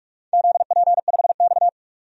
Morse code, heard as a keyed tone, about 44 wpm, 700 Hz. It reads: ZP5X
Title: DXCC-100 Call Signs - 4 Characters - 45wpm